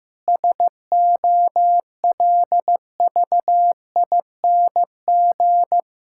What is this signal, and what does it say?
Morse code, heard as a keyed tone, about 15 wpm, 700 Hz. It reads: SOLVING
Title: Pangram Challenge v2 - 15wpm